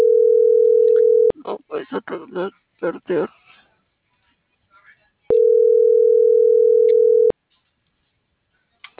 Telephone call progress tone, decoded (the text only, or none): ringback tone